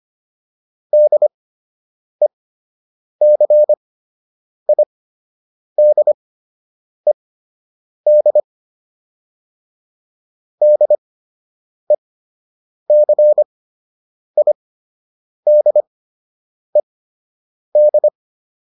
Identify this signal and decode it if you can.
Morse code, heard as a keyed tone, about 25 wpm, spaced as for 8 wpm, 600 Hz. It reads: DECIDED DECIDED